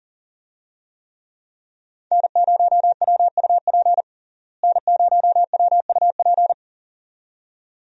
Morse code, heard as a keyed tone, about 40 wpm, 700 Hz. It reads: N0WUP N0WUP
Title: USA Call Signs - 5 Characters - Repeated 2x - 40wpm